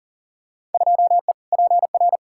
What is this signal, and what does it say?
Morse code, heard as a keyed tone, about 40 wpm, 700 Hz. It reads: 2E PR
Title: Field Day Exchanges Encore - 40wpm